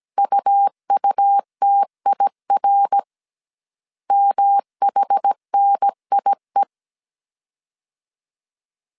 Morse code, heard as a keyed tone, about 17 wpm, 800 Hz. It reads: UUTIR MHNIE